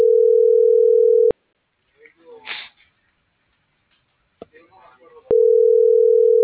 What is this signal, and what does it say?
Ringback tone (North America), 440 + 480 Hz.